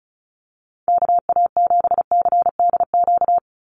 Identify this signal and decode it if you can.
Morse code, heard as a keyed tone, about 34 wpm, 700 Hz. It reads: KA7CDQ
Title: USA Call Signs 35wpm